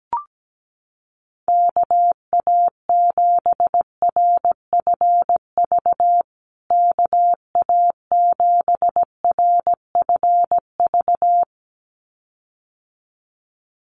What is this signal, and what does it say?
Morse code, heard as a keyed tone, about 17 wpm, 700 Hz. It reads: KA7RFV KA7RFV
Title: USA Call Signs Repeated 2x 17wpm